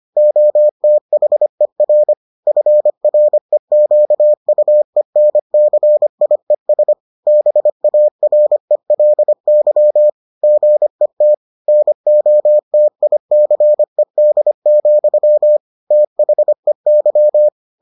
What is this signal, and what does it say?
Morse code, heard as a keyed tone, about 25 wpm, 600 Hz. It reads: OTHER FREQUENCIES BARELY GET NOTICED, THEY